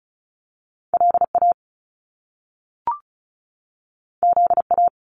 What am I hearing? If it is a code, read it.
Morse code, heard as a keyed tone, about 36 wpm, 700 Hz. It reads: L A Z A
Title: Top Two Letter Combinations in Words 35wpm